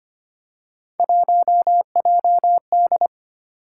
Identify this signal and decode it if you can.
Morse code, heard as a keyed tone, about 25 wpm, 700 Hz. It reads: 1JD
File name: Three Letter-Number Combinations - 25wpm